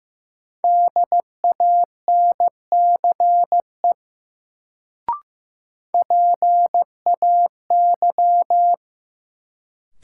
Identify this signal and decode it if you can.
Morse code, heard as a keyed tone, about 15 wpm, 700 Hz. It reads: DANCE PAY
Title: Sentences from Top 700 Words - New Word Drill - 15wpm